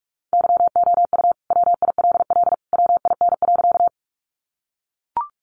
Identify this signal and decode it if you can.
Morse code, heard as a keyed tone, about 44 wpm, 700 Hz. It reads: YOU WILL WIN.
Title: Sentences from Top 1500 Words 45wpm